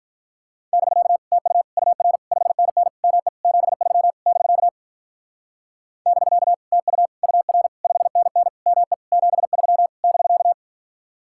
Morse code, heard as a keyed tone, about 45 wpm, 700 Hz. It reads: <BK> TU UR 5NN ME 73 <BK> <BK> TU UR 5NN ME 73 <BK>